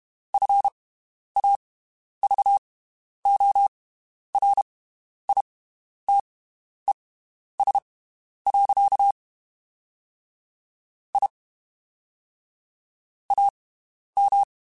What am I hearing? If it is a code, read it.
Morse code, heard as a keyed tone, about 32 wpm, spaced as for 11 wpm, 800 Hz. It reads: FAVORITES. I AM
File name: (spaced) / QSOs Spaced-mp3s / WD0FBT 1 @30 sp